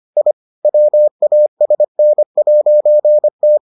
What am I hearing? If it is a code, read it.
Morse code, heard as a keyed tone, about 25 wpm, 600 Hz. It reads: I WASN'T